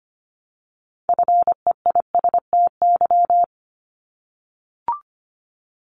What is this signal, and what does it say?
Morse code, heard as a keyed tone, about 25 wpm, 700 Hz. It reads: FEISTY
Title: Multi-letter Phonemes Repeated 3x 25wpm